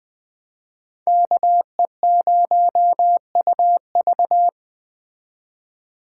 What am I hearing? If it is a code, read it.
Morse code, heard as a keyed tone, about 20 wpm, 700 Hz. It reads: KE0UV
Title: Speed Racing - USA 2x2 Call Signs 17wpm